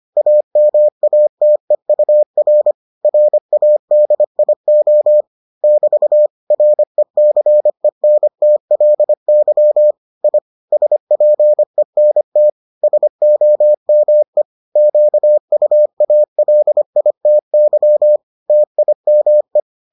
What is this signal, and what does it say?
Morse code, heard as a keyed tone, about 25 wpm, 600 Hz. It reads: AMATEUR RADIO <BT> RECENTLY I SPENT SOME QUALITY TIME